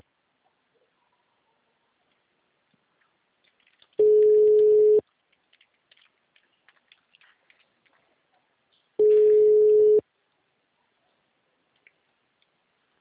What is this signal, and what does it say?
Ringback tone (Europe, 425 Hz).
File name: out- -2004- - - .57127